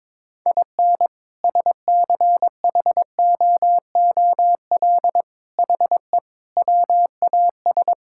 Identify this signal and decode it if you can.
Morse code, heard as a keyed tone, about 22 wpm, 700 Hz. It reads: IN SCHOOL HE WAS
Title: Sentences from Top 1000 Words - Review Part 5 - 22wpm